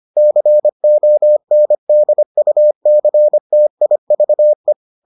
Morse code, heard as a keyed tone, about 25 wpm, 600 Hz. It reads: CONDUCTIVE